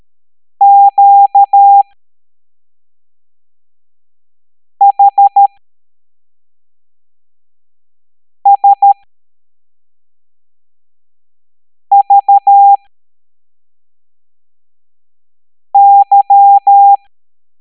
Morse code, heard as a keyed tone, about 13 wpm, spaced as for 3 wpm, 800 Hz. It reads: QHSVY
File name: hamdata / L07-EX02-HQP